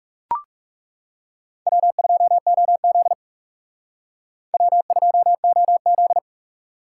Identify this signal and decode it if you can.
Morse code, heard as a keyed tone, about 36 wpm, 700 Hz. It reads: W2OZ W2OZ
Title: Speed Racing - USA 1x2 Call Signs 30wpm